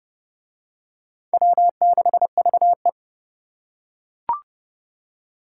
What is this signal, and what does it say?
Morse code, heard as a keyed tone, about 26 wpm, 700 Hz. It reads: W6VE